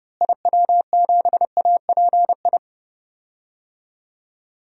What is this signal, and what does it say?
Morse code, heard as a keyed tone, about 30 wpm, 700 Hz. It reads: IW7API
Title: DXCC-100 Call Signs - 6 Characters - Repeated 3x - 30wpm